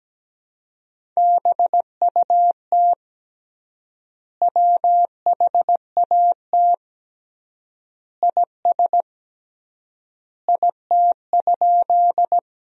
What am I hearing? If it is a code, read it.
Morse code, heard as a keyed tone, about 17 wpm, 700 Hz. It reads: BUT WHAT IS IT?
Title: 3x Word-Spacing - Sentences from Top 100 Words - 17wpm